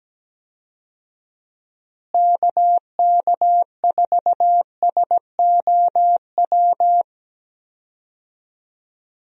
Morse code, heard as a keyed tone, about 17 wpm, 700 Hz. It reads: KK4SOW
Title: USA Call Signs - 6 Characters - Repeated 2x - 17wpm